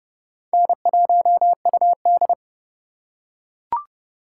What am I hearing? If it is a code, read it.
Morse code, heard as a keyed tone, about 30 wpm, 700 Hz. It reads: N1UD
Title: USA Call Signs - 4 Characters - 4x Character Spacing - 30wpm